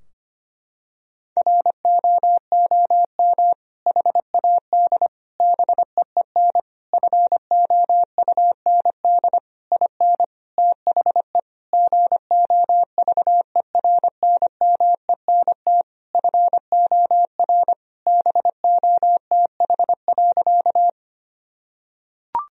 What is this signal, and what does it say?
Morse code, heard as a keyed tone, about 25 wpm, 700 Hz. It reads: ROOM HAD BEEN FOUND IN THE GOVERNMENT FOR BOTH.
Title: Sentences from Top 1000 Words - Review Part 3 - 25wpm